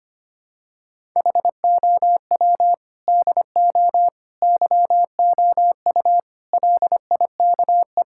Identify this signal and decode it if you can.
Morse code, heard as a keyed tone, about 25 wpm, 700 Hz. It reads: HOW DO YOU LIKE
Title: Sentences from Top 1000 Words - Review Part 6 - 25wpm